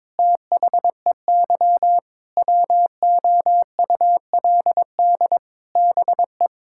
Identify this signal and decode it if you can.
Morse code, heard as a keyed tone, about 22 wpm, 700 Hz. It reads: THEY WOULD BE
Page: morse-code-ninja-practice-22wpm